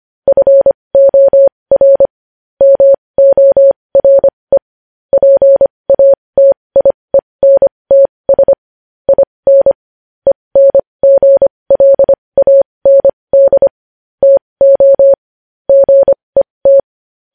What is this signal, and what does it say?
Morse code, heard as a keyed tone, about 25 wpm, spaced as for 20 wpm, 550 Hz. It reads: FOR MORE PATIENTS IN ENGLAND TO GET
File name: podcast / News headlines-1x-25x20